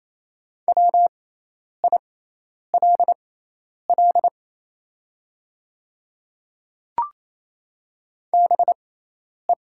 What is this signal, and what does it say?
Morse code, heard as a keyed tone, about 28 wpm, spaced as for 10 wpm, 700 Hz. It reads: WILL BE
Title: 6x Character Spacing - Top 100 Words - 28wpm